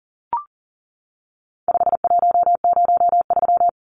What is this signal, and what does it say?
Morse code, heard as a keyed tone, about 38 wpm, 700 Hz. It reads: 5103
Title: Numbers - 4-Digit 40wpm